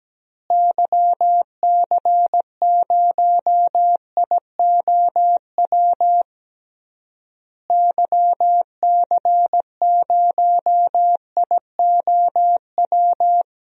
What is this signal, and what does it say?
Morse code, heard as a keyed tone, about 17 wpm, 700 Hz. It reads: YC0IOW YC0IOW
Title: DXCC-100 Call Signs - 6 Characters - Repeated 3x - 17wpm